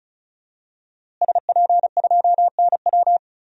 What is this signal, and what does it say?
Morse code, heard as a keyed tone, about 36 wpm, 700 Hz. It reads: SP2NW
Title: DXCC-100 Call Signs - 5 Characters - Repeated 2x - 35wpm